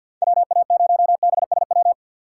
Morse code, heard as a keyed tone, about 53 wpm, 700 Hz. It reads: WA0BSW
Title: USA Call Signs - 2x Character Spacing - 50wpm